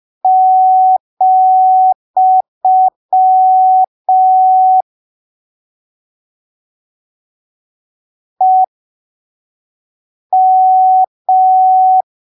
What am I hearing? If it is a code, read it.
Morse code, heard as a keyed tone, about 5 wpm, spaced as for 3 wpm, 750 Hz. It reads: , EM
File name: arquivos / texto / texto 22 05ppm